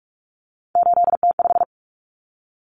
Morse code, heard as a keyed tone, about 44 wpm, 700 Hz. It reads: 8T5
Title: Common Cut Numbers - 3-Digit 45wpm